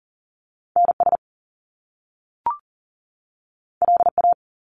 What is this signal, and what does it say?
Morse code, heard as a keyed tone, about 40 wpm, 700 Hz. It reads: NS LA
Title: Top Two Letter Combinations in Words 40wpm